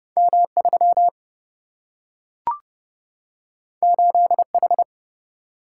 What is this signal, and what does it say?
Morse code, heard as a keyed tone, about 30 wpm, 700 Hz. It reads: M 3 8 H
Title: Single-Letter Single-Number 30wpm